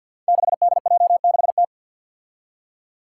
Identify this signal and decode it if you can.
Morse code, heard as a keyed tone, about 52 wpm, 700 Hz. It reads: 6DJ6T
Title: Five Challenging Letter-Number Combinations 50wpm